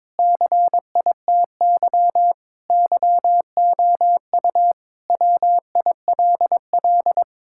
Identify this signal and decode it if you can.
Morse code, heard as a keyed tone, about 22 wpm, 700 Hz. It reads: CITY YOU WILL
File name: practice / 22wpm / Sentences with 4-Word Phrases - 22wpm